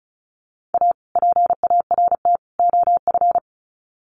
Morse code, heard as a keyed tone, about 34 wpm, 700 Hz. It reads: A PART OF
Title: Sets of 3 Words 35wpm